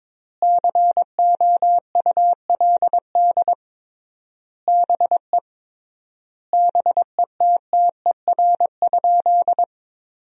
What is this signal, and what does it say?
Morse code, heard as a keyed tone, about 22 wpm, 700 Hz. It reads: COULD BE BETTER?
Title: 3x Word-Spacing - Sentences from Top 400 Words - 22wpm